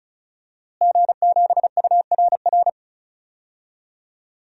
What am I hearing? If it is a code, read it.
Morse code, heard as a keyed tone, about 35 wpm, 700 Hz. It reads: G7URR
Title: DXCC-100 Call Signs - 5 Characters - Repeated 3x - 35wpm